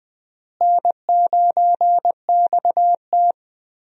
Morse code, heard as a keyed tone, about 20 wpm, 700 Hz. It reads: N9XT
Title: USA Call Signs - 4 Characters - Repeated 3x - 20wpm